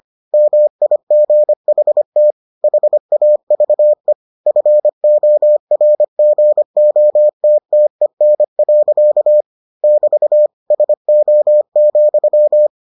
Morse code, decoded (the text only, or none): MIGHT HAVE FORGOTTEN. <BT> SO,